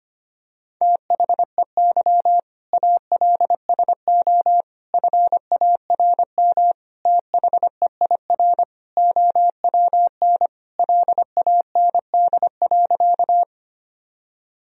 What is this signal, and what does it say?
Morse code, heard as a keyed tone, about 25 wpm, 700 Hz. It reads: THEY ALSO FARM THEIR OWN LAND.
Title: Sentences from Top 1000 Words - Review Part 2 - 25wpm